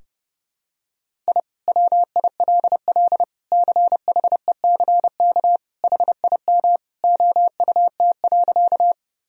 Morse code, heard as a keyed tone, about 32 wpm, 700 Hz. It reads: I WILL CHECK HIM OUT.
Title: Sentences from Top 1000 Words - Review Part 3 - 30wpm